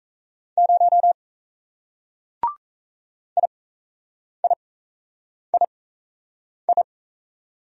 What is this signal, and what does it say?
Morse code, heard as a keyed tone, about 36 wpm, spaced as for 8 wpm, 700 Hz. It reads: 0 IIII